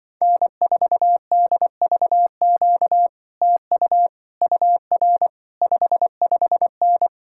Morse code, heard as a keyed tone, about 23 wpm, 700 Hz. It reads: N4DVQ TU UR 55N
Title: Speed-Racing Common QSO Phrases - 20wpm